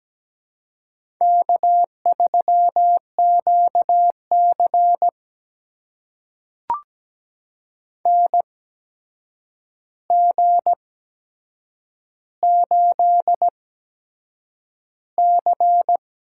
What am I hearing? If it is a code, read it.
Morse code, heard as a keyed tone, about 17 wpm, spaced as for 7 wpm, 700 Hz. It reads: K3QC NG8C
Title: USA Call Signs - 4 Characters - 8x Character Spacing - 17wpm